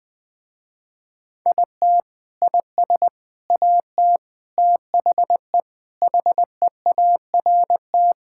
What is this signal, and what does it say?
Morse code, heard as a keyed tone, about 20 wpm, 700 Hz. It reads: IT IS AT THE HEART
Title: Sentences with 4-Word Phrases - Spoken Beginning - 20wpm